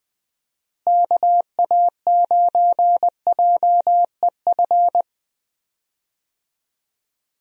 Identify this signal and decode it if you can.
Morse code, heard as a keyed tone, about 21 wpm, 700 Hz. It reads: KA9JEF
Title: USA Call Signs - 6 Characters - Repeated 2x - 20wpm